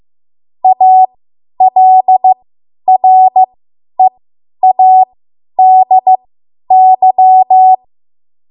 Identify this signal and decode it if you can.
Morse code, heard as a keyed tone, about 15 wpm, spaced as for 10 wpm, 750 Hz. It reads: ALREADY